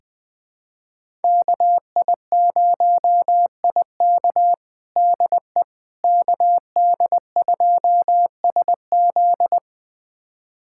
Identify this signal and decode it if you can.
Morse code, heard as a keyed tone, about 20 wpm, 700 Hz. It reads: KI0IK DE KD2SZ
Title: USA Callsign DE Callsign - 5 Characters - 20wpm